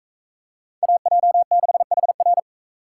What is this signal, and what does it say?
Morse code, heard as a keyed tone, about 44 wpm, 700 Hz. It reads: AJ6HR